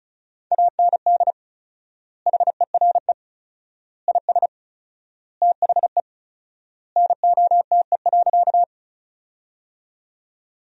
Morse code, heard as a keyed tone, about 34 wpm, 700 Hz. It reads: AND HERE IS THE NOTE.
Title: 4x Word-Spacing - Sentences from Top 500 Words - 35wpm